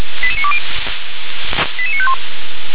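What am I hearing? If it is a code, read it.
Five-tone selective call (ZVEI), heard as quick runs of five tones, tones of about 70 ms. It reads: 88920 88951
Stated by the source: ZVEI five-tone call received by SDR.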